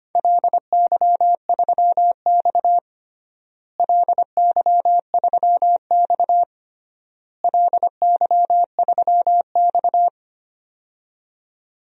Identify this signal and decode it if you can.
Morse code, heard as a keyed tone, about 25 wpm, 700 Hz. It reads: LY3X LY3X LY3X
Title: DXCC-100 Call Signs - 4 Characters - Repeated 3x - 25wpm